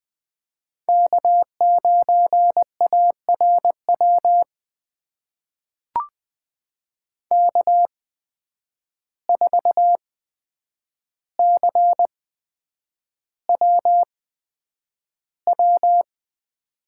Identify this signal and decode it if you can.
Morse code, heard as a keyed tone, about 20 wpm, spaced as for 9 wpm, 700 Hz. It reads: K9ARW K4CWW
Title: USA Call Signs - 5 Characters - 8x Character Spacing - 20wpm